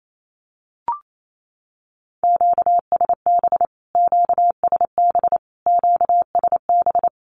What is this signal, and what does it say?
Morse code, heard as a keyed tone, about 28 wpm, 700 Hz. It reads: QSB QSB QSB
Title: Q-Codes Repeated 3x 28wpm